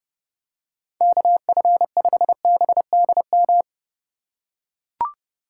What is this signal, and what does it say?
Morse code, heard as a keyed tone, about 30 wpm, 700 Hz. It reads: KF5BDM
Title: USA Call Signs - 6 Characters - 2x Character Spacing - 30wpm